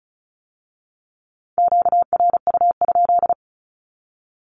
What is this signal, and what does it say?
Morse code, heard as a keyed tone, about 36 wpm, 700 Hz. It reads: QRU?
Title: Q-Codes with Spoken Meaning - 35wpm